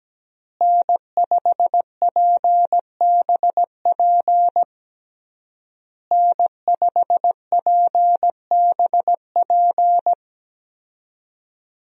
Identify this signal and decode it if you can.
Morse code, heard as a keyed tone, about 17 wpm, 700 Hz. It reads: N5PBP N5PBP
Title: USA Call Signs - 5 Characters - Repeated 3x - 17wpm